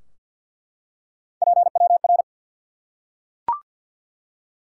Morse code, heard as a keyed tone, about 51 wpm, 700 Hz. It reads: PWR
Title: Top 90 QSO Elements 50wpm